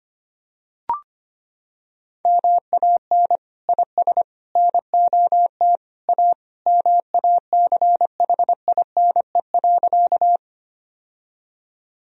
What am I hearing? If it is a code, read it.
Morse code, heard as a keyed tone, about 25 wpm, 700 Hz. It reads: MAN IS NOT A MACHINE.
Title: Sentences from Top 1000 Words - Review Part 2 - 25wpm